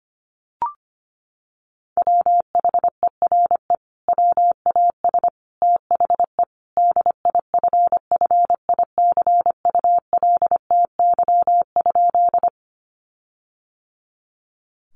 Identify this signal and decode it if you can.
Morse code, heard as a keyed tone, about 25 wpm, 700 Hz. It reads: WHERE WAS THE DIFFICULTY?
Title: Sentences from Top 1500 Words 25wpm